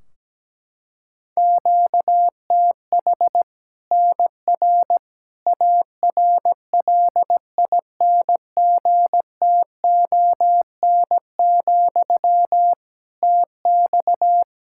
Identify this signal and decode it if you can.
Morse code, heard as a keyed tone, about 17 wpm, 700 Hz. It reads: QTH NR ARLINGTON, TX
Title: Practice QSO - Exchange 1 - Beginning - 17wpm